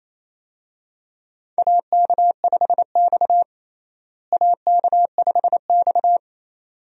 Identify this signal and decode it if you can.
Morse code, heard as a keyed tone, about 27 wpm, 700 Hz. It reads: AK5X AK5X
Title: USA Call Signs - 4 Characters - Repeated 2x - 28wpm